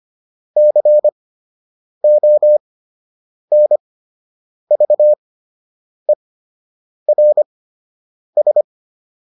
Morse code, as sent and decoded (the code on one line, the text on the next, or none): -.-. --- -. ...- . .-. ...
CONVERS